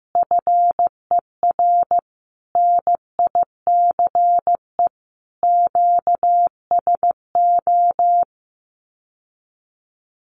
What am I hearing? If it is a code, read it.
Morse code, heard as a keyed tone, about 15 wpm, 700 Hz. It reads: FER NICE QSO
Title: Common QSO Phrases Encore 15wpm